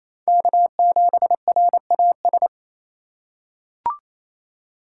Morse code, heard as a keyed tone, about 27 wpm, 700 Hz. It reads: K7RAS
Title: USA Call Signs - 5 Characters - 4x Character Spacing - 28wpm